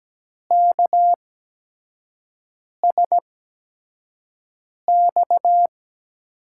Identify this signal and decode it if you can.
Morse code, heard as a keyed tone, about 17 wpm, spaced as for 5 wpm, 700 Hz. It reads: KSX
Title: USA Call Signs - 6 Characters - 8x Character Spacing - 17wpm